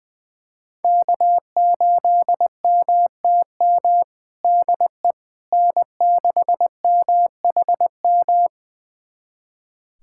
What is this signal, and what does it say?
Morse code, heard as a keyed tone, about 20 wpm, 700 Hz. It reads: K8MTM DE N6MHM
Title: USA Callsign DE Callsign - 5 Characters - 20wpm